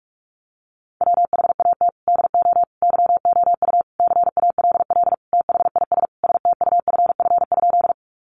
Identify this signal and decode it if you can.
Morse code, heard as a keyed tone, about 44 wpm, 700 Hz. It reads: WHAT DO YOU CALL THIS STUFF?
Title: Sentences from Top 1400 Words 45wpm